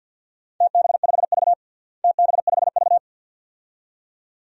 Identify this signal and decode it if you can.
Morse code, heard as a keyed tone, about 47 wpm, 700 Hz. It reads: TB5V TB5V
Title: DXCC-100 Call Signs - 4 Characters - Repeated 2x - 50wpm